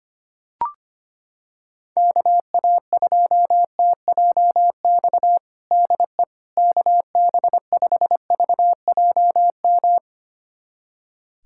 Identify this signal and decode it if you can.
Morse code, heard as a keyed tone, about 25 wpm, 700 Hz. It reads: KA2TJX DE KB5VJM
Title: USA Callsign DE Callsign 25wpm